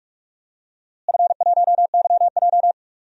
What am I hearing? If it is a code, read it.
Morse code, heard as a keyed tone, about 46 wpm, 700 Hz. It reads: F1YJ
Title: DXCC-100 Call Signs - 4 Characters - Repeated 2x - 45wpm